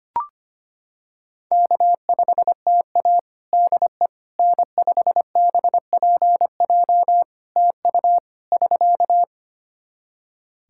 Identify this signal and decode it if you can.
Morse code, heard as a keyed tone, about 25 wpm, 700 Hz. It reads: K5TA DE N5BPJ TU <SK>
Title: K1USN SST QSO Practice 25wpm